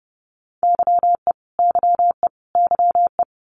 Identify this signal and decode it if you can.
Morse code, heard as a keyed tone, about 30 wpm, 700 Hz. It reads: YE YE YE
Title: Multi-letter Phonemes Repeated 3x 30wpm